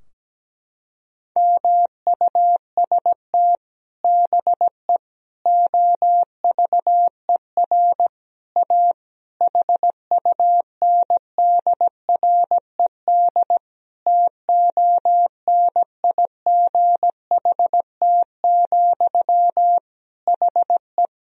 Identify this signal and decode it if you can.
Morse code, heard as a keyed tone, about 17 wpm, 700 Hz. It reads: MUST BE OVER A HUNDRED TONIGHT, HE
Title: Sentences from Top 1000 Words - Review Part 2 - 17wpm